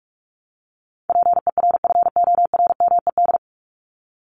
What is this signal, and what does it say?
Morse code, heard as a keyed tone, about 44 wpm, 700 Hz. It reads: PERFORMED
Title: ED Suffix - Top 100 Words with Base Words 45wpm